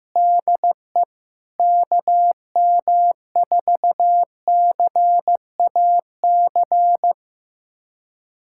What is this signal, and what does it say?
Morse code, heard as a keyed tone, about 15 wpm, 700 Hz. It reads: DE KM4CAC
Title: USA Callsign DE Callsign 15wpm